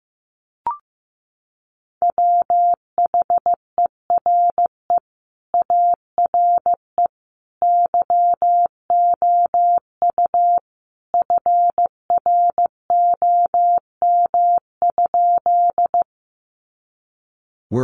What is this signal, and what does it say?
Morse code, heard as a keyed tone, about 15 wpm, 700 Hz. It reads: WHERE ARE YOU FROM?
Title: Sentences from Top 200 Words 15wpm